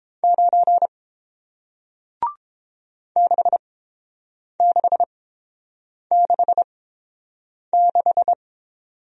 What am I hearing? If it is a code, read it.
Morse code, heard as a keyed tone, about 30 wpm, spaced as for 7 wpm, 700 Hz. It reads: 9 6666